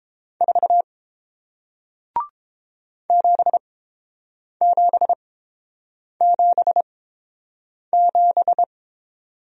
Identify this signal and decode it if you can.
Morse code, heard as a keyed tone, about 30 wpm, spaced as for 7 wpm, 700 Hz. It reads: 4 7777